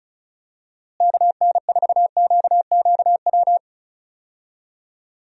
Morse code, heard as a keyed tone, about 34 wpm, 700 Hz. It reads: KN4QQW
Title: USA Call Signs - 6 Characters - Repeated 2x - 35wpm